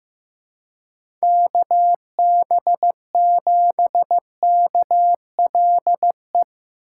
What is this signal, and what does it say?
Morse code, heard as a keyed tone, about 15 wpm, 700 Hz. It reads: KB7KLE
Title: USA Call Signs - 4x Character Spacing - 15wpm